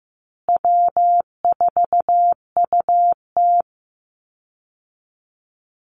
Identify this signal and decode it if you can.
Morse code, heard as a keyed tone, about 15 wpm, 700 Hz. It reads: W4UT
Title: Easy Call Signs Repeated 3x - 15wpm